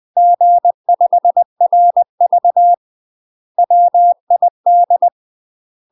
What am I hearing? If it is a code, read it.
Morse code, heard as a keyed tone, about 20 wpm, 700 Hz. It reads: G5RV WID
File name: UK MASTER QSOs TRAINING / 20WPM / UK MASTER QSOs 0067